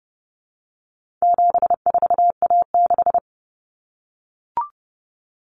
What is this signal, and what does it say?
Morse code, heard as a keyed tone, about 30 wpm, 700 Hz. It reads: 74A6
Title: Common Cut Numbers - 4-Digit 30wpm